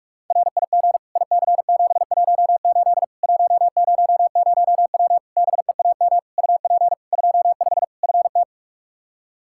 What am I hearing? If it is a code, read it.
Morse code, heard as a keyed tone, about 46 wpm, 700 Hz. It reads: RIG IC718 100W BEAM UP 25 FT
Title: Practice QSO - Exchange 2 - Info - 45wpm